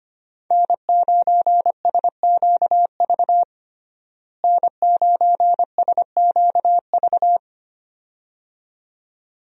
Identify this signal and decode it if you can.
Morse code, heard as a keyed tone, about 25 wpm, 700 Hz. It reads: N9SQV N9SQV
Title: USA Call Signs - 5 Characters - Repeated 2x - 25wpm